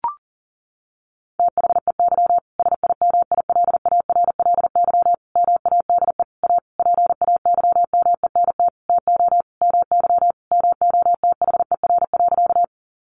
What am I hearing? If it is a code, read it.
Morse code, heard as a keyed tone, about 40 wpm, 700 Hz. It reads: THEY SIMILARLY MADE A PAYMENT TO MY MOTHER.
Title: Sentences from Top 1400 Words 40wpm